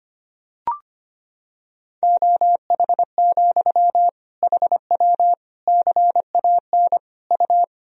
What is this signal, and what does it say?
Morse code, heard as a keyed tone, about 25 wpm, 700 Hz. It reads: OH, HW CAN U
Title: Sentences from Top 500 Words and Abbreviations 25wpm